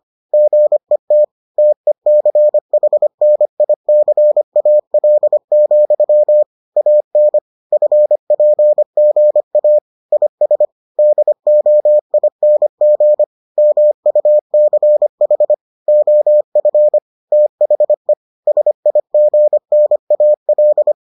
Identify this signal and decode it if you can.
Morse code, heard as a keyed tone, about 25 wpm, 600 Hz. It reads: GET TECHNICAL, AN FPGA IS DOING MUCH OF THE SIGNAL